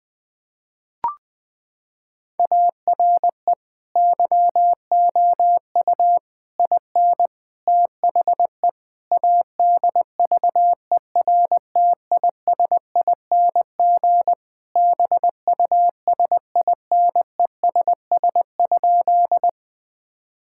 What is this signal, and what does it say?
Morse code, heard as a keyed tone, about 20 wpm, 700 Hz. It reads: ARE YOU IN THE ADVERTISING BUSINESS?
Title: Sentences from Top 1000 Words - Review Part 6 - 20wpm